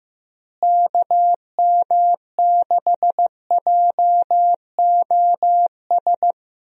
Morse code, heard as a keyed tone, about 15 wpm, 700 Hz. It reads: KM6JOS
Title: USA Call Signs - 6 Characters - Repeated 3x - 15wpm